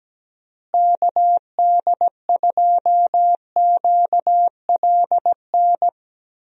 Morse code, heard as a keyed tone, about 17 wpm, 700 Hz. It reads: KD2QLN
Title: USA Call Signs - 6 Characters - Repeated 2x - 17wpm